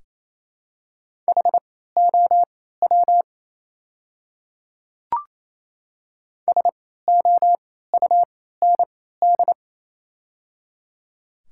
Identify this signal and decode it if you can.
Morse code, heard as a keyed tone, about 28 wpm, spaced as for 16 wpm, 700 Hz. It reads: HOW SOUND
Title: 3x Character Spacing - Top 100 Words - 28wpm